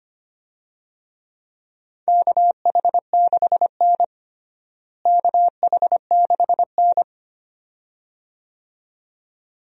Morse code, heard as a keyed tone, about 25 wpm, 700 Hz. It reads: KH6N KH6N
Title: DXCC-100 Call Signs - 4 Characters - Repeated 2x - 25wpm